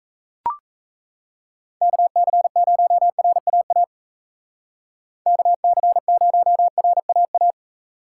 Morse code, heard as a keyed tone, about 36 wpm, 700 Hz. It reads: KC0RAA KC0RAA